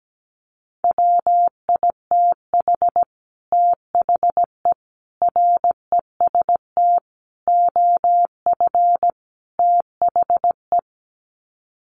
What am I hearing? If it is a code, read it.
Morse code, heard as a keyed tone, about 17 wpm, 700 Hz. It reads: WITH THE REST OF THE
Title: Sets of 5 Words 17wpm